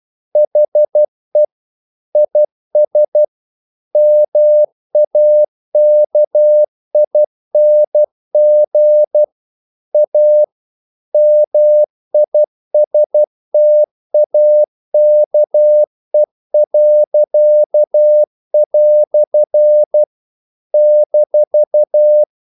Morse code, heard as a keyed tone, about 12 wpm, 600 Hz. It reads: HE IS MAKING A MISTAKE." -
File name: CW / Practice / EnglishQuotations0112